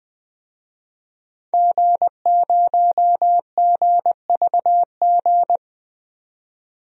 Morse code, heard as a keyed tone, about 20 wpm, 700 Hz. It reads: G0GVG
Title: DXCC-100 Call Signs - 5 Characters - 20wpm